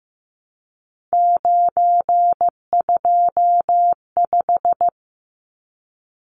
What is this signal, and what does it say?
Morse code, heard as a keyed tone, about 15 wpm, 700 Hz. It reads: 925
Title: Numbers - 3-Digit 15wpm